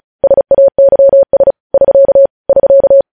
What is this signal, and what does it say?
Morse code, heard as a keyed tone, about 36 wpm, 550 Hz. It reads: SAYS <SK> <SK>
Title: News Headlines at 35